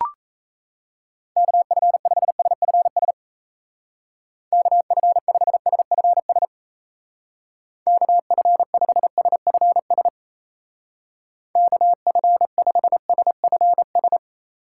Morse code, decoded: KF5SFS KF5SFS KF5SFS KF5SFS

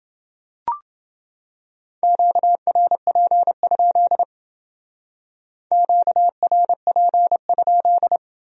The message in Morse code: QRP? QRP?

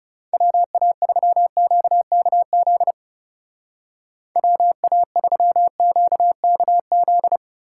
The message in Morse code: WA3QKZ WA3QKZ

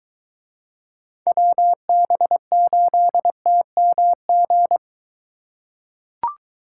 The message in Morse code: WB8TMG

